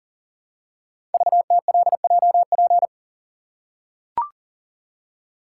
Morse code, heard as a keyed tone, about 42 wpm, 700 Hz. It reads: VTLJP
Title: Code Groups - 5 Letters - 40wpm